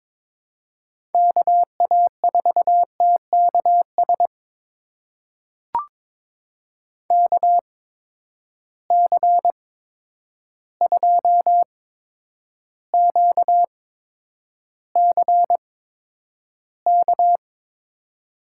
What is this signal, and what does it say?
Morse code, heard as a keyed tone, about 22 wpm, spaced as for 9 wpm, 700 Hz. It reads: KA4TKS KC2QCK
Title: USA Call Signs - 6 Characters - 8x Character Spacing - 22wpm